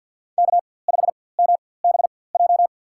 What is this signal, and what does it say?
Morse code, heard as a keyed tone, about 52 wpm, spaced as for 23 wpm, 700 Hz. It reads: K5MBJ
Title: USA Call Signs - 5 Characters - 4x Character Spacing - 50wpm